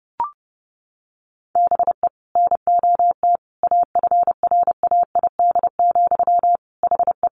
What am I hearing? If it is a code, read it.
Morse code, heard as a keyed tone, about 30 wpm, 700 Hz. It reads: BE NOT AFRAID, HE
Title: Sentences from Top 1500 Words 30wpm